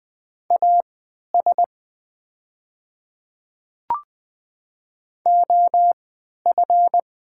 Morse code, heard as a keyed tone, about 20 wpm, spaced as for 11 wpm, 700 Hz. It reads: AS OF
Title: 3x Character Spacing - Top 100 Words - 20wpm